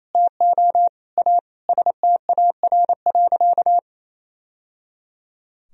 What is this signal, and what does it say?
Morse code, heard as a keyed tone, about 28 wpm, 700 Hz. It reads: TO A STAR.
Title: Sentences from Top 1000 Words - Review Part 2 - 28wpm